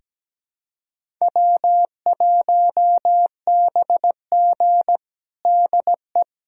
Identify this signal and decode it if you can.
Morse code, heard as a keyed tone, about 17 wpm, 700 Hz. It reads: W1BG DE
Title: USA Callsign DE Callsign - 4 Characters - 17wpm